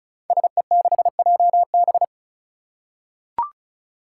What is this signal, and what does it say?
Morse code, heard as a keyed tone, about 36 wpm, 700 Hz. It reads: SE6JB
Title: Five Challenging Letter-Number Combinations 35wpm